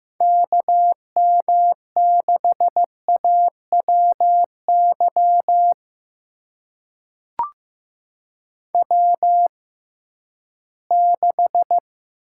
Morse code, heard as a keyed tone, about 15 wpm, spaced as for 11 wpm, 700 Hz. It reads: KM6AWY W6
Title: USA Call Signs - 6x Character Spacing - 15wpm